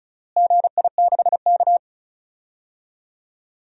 Morse code, heard as a keyed tone, about 36 wpm, 700 Hz. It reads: GI6K
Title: DXCC-100 Call Signs - 4 Characters - Repeated 2x - 35wpm